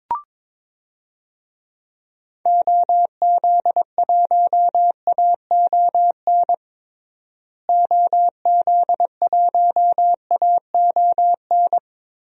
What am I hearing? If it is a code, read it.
Morse code, heard as a keyed tone, about 22 wpm, 700 Hz. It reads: OZ1AON OZ1AON
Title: DXCC-100 Call Signs - 6 Characters - Repeated 3x - 22wpm